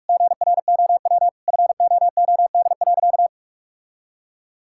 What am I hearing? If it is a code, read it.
Morse code, heard as a keyed tone, about 46 wpm, 700 Hz. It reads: GROW FOOD.
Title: Sentences from Top 1000 Words - Review Part 6 - 45wpm